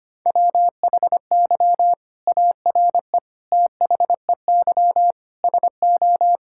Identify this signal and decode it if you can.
Morse code, heard as a keyed tone, about 25 wpm, 700 Hz. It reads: WHY ARE THEY SO
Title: Sentences from Top 1000 Words - Review Part 6 - 25wpm